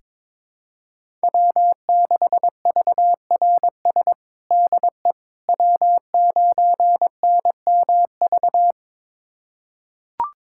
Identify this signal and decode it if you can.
Morse code, heard as a keyed tone, about 21 wpm, 700 Hz. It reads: W6VRS DE W9NMV
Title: USA Callsign DE Callsign - 5 Characters - 22wpm